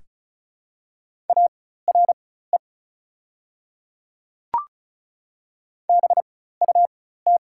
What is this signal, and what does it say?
Morse code, heard as a keyed tone, about 36 wpm, spaced as for 16 wpm, 700 Hz. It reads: ARE BUT
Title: 4x Character Spacing - Top 300 Words - 35wpm